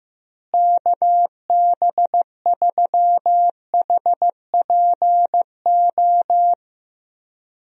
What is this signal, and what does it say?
Morse code, heard as a keyed tone, about 15 wpm, 700 Hz. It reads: KB3HPO
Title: USA Call Signs - 6 Characters - Repeated 3x - 15wpm